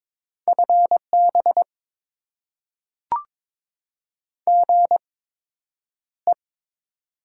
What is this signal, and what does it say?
Morse code, heard as a keyed tone, about 22 wpm, spaced as for 9 wpm, 700 Hz. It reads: FB GE
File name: Top 20 QSO Elements - 8x Character Spacing - 22wpm